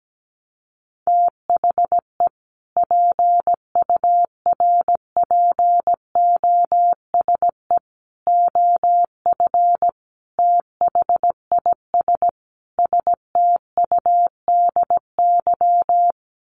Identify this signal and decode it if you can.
Morse code, heard as a keyed tone, about 17 wpm, 700 Hz. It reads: THE PURPOSE OF THIS STUDY
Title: Sets of 5 Words 17wpm